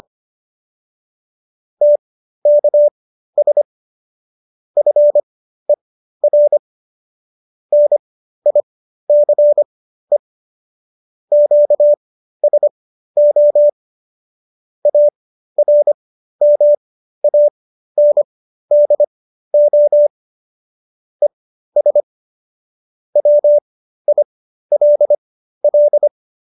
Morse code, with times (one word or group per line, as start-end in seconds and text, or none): 1.8-3.6 TKS
4.8-6.6 FER
7.7-10.2 NICE
11.3-13.7 QSO
14.9-20.1 ARMANDO
21.2-22.0 ES
23.2-26.1 WILL